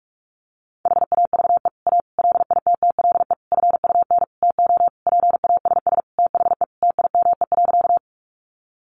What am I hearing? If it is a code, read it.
Morse code, heard as a keyed tone, about 44 wpm, 700 Hz. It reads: HAVE A LITTLE FUN TO PASS THE TIME.
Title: Sentences from Top 700 Words 45wpm